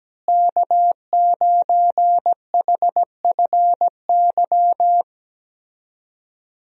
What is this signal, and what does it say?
Morse code, heard as a keyed tone, about 17 wpm, 700 Hz. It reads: K9HFY
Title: USA Call Signs - 5 Characters - Repeated 2x - 17wpm